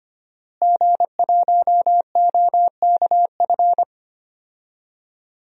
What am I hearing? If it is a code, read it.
Morse code, heard as a keyed tone, about 25 wpm, 700 Hz. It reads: G1OKF
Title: DXCC-100 Call Signs - 5 Characters - Repeated 2x - 25wpm